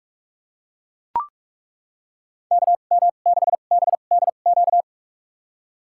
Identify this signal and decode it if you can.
Morse code, heard as a keyed tone, about 42 wpm, spaced as for 31 wpm, 700 Hz. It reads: KM6BDQ
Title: USA Call Signs - 6 Characters - 2x Character Spacing - 45wpm